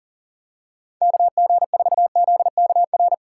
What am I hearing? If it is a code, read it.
Morse code, heard as a keyed tone, about 38 wpm, 700 Hz. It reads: KG4ZKR
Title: USA Call Signs - 6 Characters - 3x Character Spacing - 40wpm